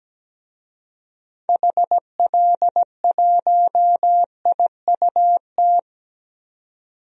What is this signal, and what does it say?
Morse code, heard as a keyed tone, about 17 wpm, 700 Hz. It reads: HL1IUT
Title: DXCC-100 Call Signs - 6 Characters - 17wpm